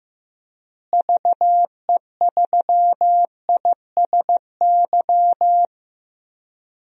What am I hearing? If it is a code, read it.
Morse code, heard as a keyed tone, about 16 wpm, 700 Hz. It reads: VE3ISY
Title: DXCC-100 Call Signs - 6 Characters - Repeated 2x - 15wpm